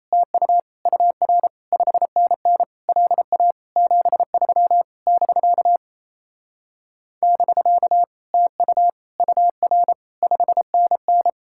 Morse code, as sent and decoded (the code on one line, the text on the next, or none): - ..- / ..- .-. / ..... -. -. / .-.. .- / --... ...-- / -...-.- / -...-.- / - ..- / ..- .-. / ..... -. -.
TU UR 5NN LA 73 <BK> <BK> TU UR 5NN